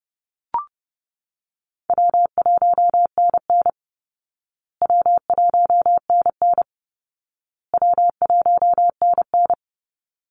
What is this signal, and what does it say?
Morse code, heard as a keyed tone, about 30 wpm, 700 Hz. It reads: W1NN W1NN W1NN
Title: Easy Call Signs Repeated 3x - 30wpm